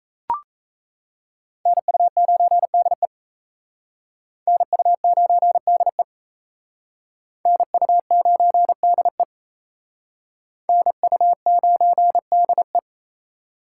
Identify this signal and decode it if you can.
Morse code, heard as a keyed tone, about 35 wpm, 700 Hz. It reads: NU9DE NU9DE NU9DE NU9DE